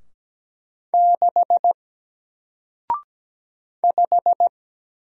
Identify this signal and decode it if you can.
Morse code, heard as a keyed tone, about 17 wpm, 700 Hz. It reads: 6 5
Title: Single Character (Letter:Number:Punctuation:Prosign) Alt 17wpm